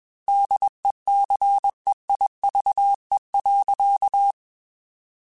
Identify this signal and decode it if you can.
Morse code, heard as a keyed tone, about 21 wpm, 800 Hz. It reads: DECEIVE.
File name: Quotes / Quotes-mp3s / Deceive @20